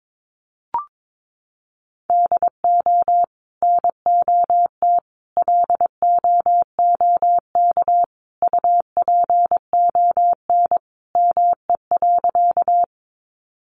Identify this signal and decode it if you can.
Morse code, heard as a keyed tone, about 22 wpm, 700 Hz. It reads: DO NOT LOOK UPON ME.
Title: Sentences from Top 1100 Words 22wpm